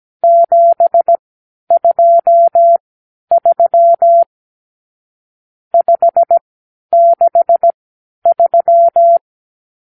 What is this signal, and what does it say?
Morse code, heard as a keyed tone, about 17 wpm, spaced as for 10 wpm, 680 Hz. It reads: 723 563
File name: k7qo / Lesson 52 (8)